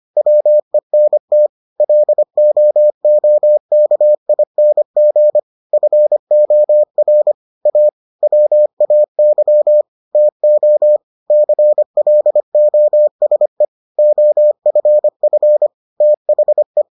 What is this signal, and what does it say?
Morse code, heard as a keyed tone, about 25 wpm, 600 Hz. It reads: WENT LOOKING FOR A WAY TO CLOSE OFF THE